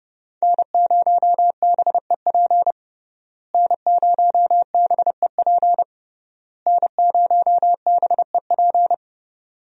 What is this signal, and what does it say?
Morse code, heard as a keyed tone, about 30 wpm, 700 Hz. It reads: N0BEP N0BEP N0BEP
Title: USA Call Signs - 5 Characters - Repeated 3x - 30wpm